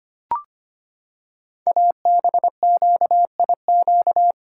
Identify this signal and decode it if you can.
Morse code, heard as a keyed tone, about 25 wpm, 700 Hz. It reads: ABQIQ
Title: Code Groups - 5 Letters - 25wpm